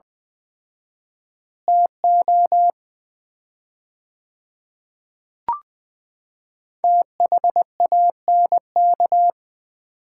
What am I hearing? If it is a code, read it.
Morse code, heard as a keyed tone, about 21 wpm, 700 Hz. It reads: TO THANK
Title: 4x Word-Spacing - Sets of 5 Words - 20wpm